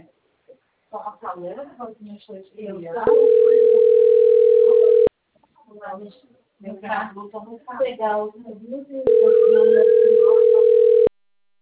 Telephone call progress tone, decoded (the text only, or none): ringback tone